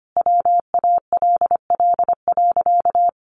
Morse code, heard as a keyed tone, about 25 wpm, 700 Hz. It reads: WALL.
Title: Sentences from Top 1500 Words 25wpm